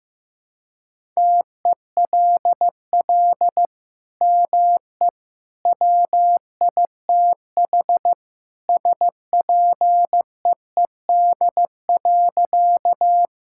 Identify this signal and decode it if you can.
Morse code, heard as a keyed tone, about 15 wpm, 700 Hz. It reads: TELL ME WITH SPEED.
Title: Sentences from Top 1000 Words - Review Part 5 - 15wpm